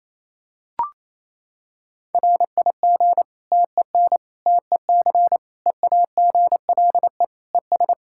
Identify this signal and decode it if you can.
Morse code, heard as a keyed tone, about 27 wpm, 700 Hz. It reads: RIG TEN TEC EAGLE ES
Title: Practice QSO - Exchange 2 - Info - 28wpm